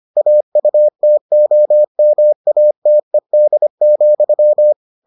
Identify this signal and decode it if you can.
Morse code, heard as a keyed tone, about 25 wpm, 600 Hz. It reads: AUTOMATED,